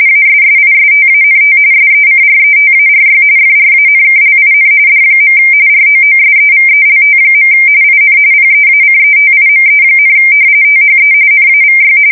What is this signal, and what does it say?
RTTY, about 45.45 baud. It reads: ASSISTANT SECRETARY FRANKLIN D. ROOSEVELT. IN PEACETIME, THE NAVY
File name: fsk170 / usnavy4